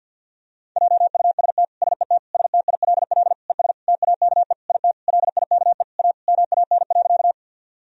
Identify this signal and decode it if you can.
Morse code, heard as a keyed tone, about 50 wpm, 700 Hz. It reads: JUST SET STILL ES TAKE IT LIKE A MAN.
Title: Sentences from Top 300 Words and Abbreviations 50wpm